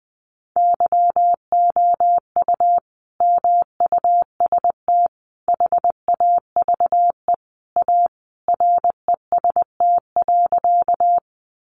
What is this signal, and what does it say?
Morse code, heard as a keyed tone, about 19 wpm, 700 Hz. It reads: YOU MUST HAVE A REST.
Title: Sentences from Top 500 Words 20wpm